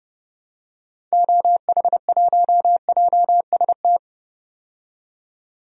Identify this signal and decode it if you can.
Morse code, heard as a keyed tone, about 30 wpm, 700 Hz. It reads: OH1JST
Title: DXCC-100 Call Signs - 6 Characters - 30wpm